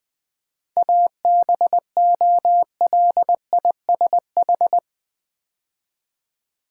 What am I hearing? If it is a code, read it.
Morse code, heard as a keyed tone, about 20 wpm, 700 Hz. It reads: ABOLISH
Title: 7 Letter Words - First 3 Letters Given - 20wpm